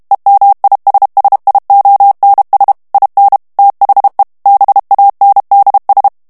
Morse code, heard as a keyed tone, about 32 wpm, 800 Hz. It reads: EMISSIONS IN THE BANDS
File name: morse mp3 / ANGLAIS texte clair 3 vitesse 30 mots minute